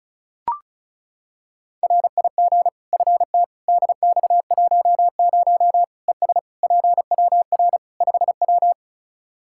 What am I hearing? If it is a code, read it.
Morse code, heard as a keyed tone, about 34 wpm, 700 Hz. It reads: RIG FT DX10 ES PWR 5W
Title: Practice QSO - Exchange 2 - Info - 35wpm